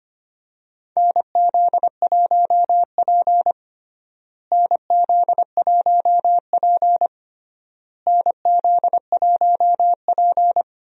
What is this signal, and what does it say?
Morse code, heard as a keyed tone, about 25 wpm, 700 Hz. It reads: NZ1P NZ1P NZ1P
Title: USA Call Signs - 4 Characters - Repeated 3x - 25wpm